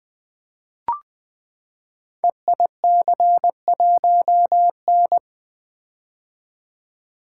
Morse code, heard as a keyed tone, about 20 wpm, 700 Hz. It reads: EIC1N
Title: Code Groups - 5 Letter:Numbers - 20wpm